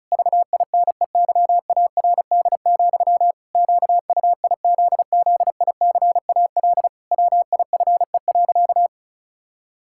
Morse code, heard as a keyed tone, about 36 wpm, 700 Hz. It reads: VINEYARD, QUIZZICAL WIFE.
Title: Pangram Challenge v2 - 35wpm